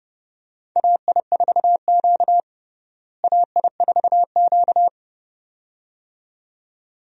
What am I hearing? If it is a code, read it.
Morse code, heard as a keyed tone, about 30 wpm, 700 Hz. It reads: AI4Q AI4Q
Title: USA Call Signs - 4 Characters - Repeated 2x - 30wpm